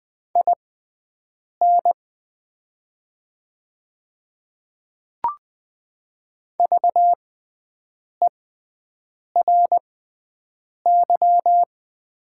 Morse code, heard as a keyed tone, about 20 wpm, spaced as for 7 wpm, 700 Hz. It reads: IN VERY
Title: 6x Character Spacing - Top 300 Words - 20wpm